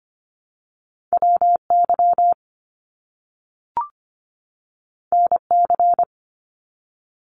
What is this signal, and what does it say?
Morse code, heard as a keyed tone, about 25 wpm, 700 Hz. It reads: WY NC
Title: US States 25wpm